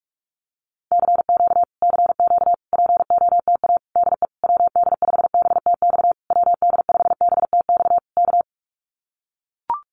Morse code, heard as a keyed tone, about 44 wpm, 700 Hz. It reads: CQ CQ POTA DE WD5BTX WD5BTX K
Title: Common QSO Phrases Encore 45wpm